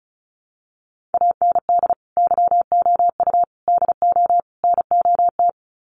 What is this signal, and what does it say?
Morse code, heard as a keyed tone, about 34 wpm, 700 Hz. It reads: AND YOU DO NOT
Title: Sets of 4 Words 35wpm